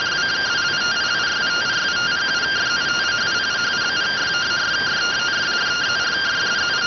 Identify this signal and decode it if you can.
Two-tone teleprinter signal, about 170 Hz shift.